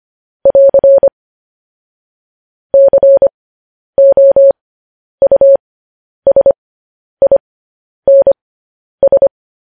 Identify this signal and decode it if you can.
Morse code, heard as a keyed tone, about 25 wpm, spaced as for 10 wpm, 550 Hz. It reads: <AR> COUSINS
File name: podcast / News headlines-1x-25x10